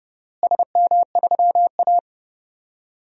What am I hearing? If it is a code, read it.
Morse code, heard as a keyed tone, about 30 wpm, 700 Hz. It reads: SM3A
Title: DXCC-100 Call Signs - 4 Characters - Repeated 3x - 30wpm